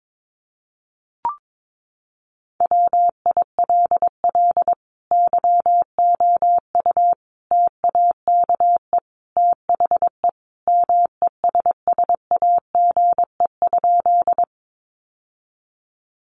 Morse code, heard as a keyed tone, about 22 wpm, 700 Hz. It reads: WILL YOU TAKE THE MESSAGE?
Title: Sentences from Top 1100 Words 22wpm